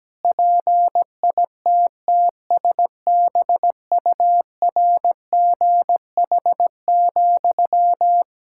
Morse code, heard as a keyed tone, about 17 wpm, 700 Hz. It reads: PITTSBURGH,
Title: Practice QSO - Exchange 1 - Beginning - 17wpm